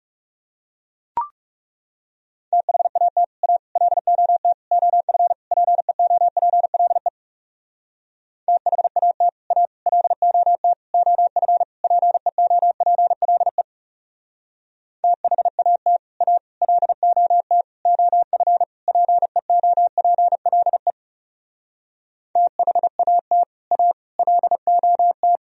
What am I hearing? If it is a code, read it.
Morse code, heard as a keyed tone, about 36 wpm, 700 Hz. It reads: THAT A LOT OF PEOPLE THAT A LOT OF PEOPLE THAT A LOT OF PEOPLE THAT A LOT
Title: Speed-Racing - Sets of 5 Words 30wpm